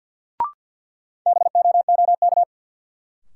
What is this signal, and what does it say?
Morse code, heard as a keyed tone, about 52 wpm, 700 Hz. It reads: BOOK
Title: Sentences from Top 300 Words - New Word Drill - 50wpm